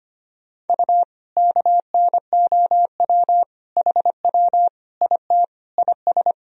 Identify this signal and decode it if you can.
Morse code, heard as a keyed tone, about 25 wpm, 700 Hz. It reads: U KNOW HW IT IS
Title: Sentences from Top 100 Words and Abbreviations 25wpm